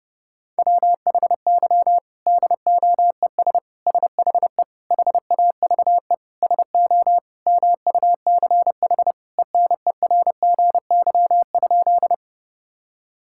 Morse code, heard as a keyed tone, about 32 wpm, 700 Hz. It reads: WHY DOES SHE HAVE SO MUCH ENERGY?
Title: Sentences from Top 1000 Words - Review Part 6 - 30wpm